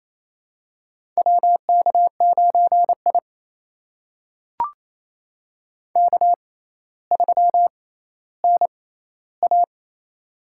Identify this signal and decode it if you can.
Morse code, heard as a keyed tone, about 28 wpm, spaced as for 14 wpm, 700 Hz. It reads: WK9I K3NA
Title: USA Call Signs - 4 Characters - 6x Character Spacing - 28wpm